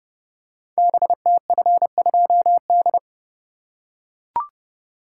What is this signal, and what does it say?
Morse code, heard as a keyed tone, about 30 wpm, 700 Hz. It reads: BTF2D
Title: Code Groups - 5 Letter:Numbers - 30wpm